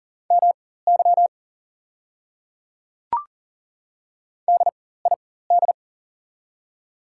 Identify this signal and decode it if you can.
Morse code, heard as a keyed tone, about 37 wpm, spaced as for 18 wpm, 690 Hz. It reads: MY DID
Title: 4x Character Spacing - Top 100 Words - 40wpm